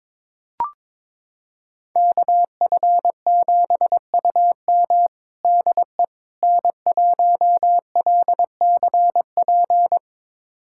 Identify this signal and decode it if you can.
Morse code, heard as a keyed tone, about 22 wpm, 700 Hz. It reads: KF7UM DE N1LCP
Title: USA Callsign DE Callsign - 5 Characters - 22wpm